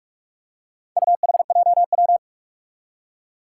Morse code, heard as a keyed tone, about 47 wpm, 700 Hz. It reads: UHJW
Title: Four Challenging Letter-Number Combinations 45wpm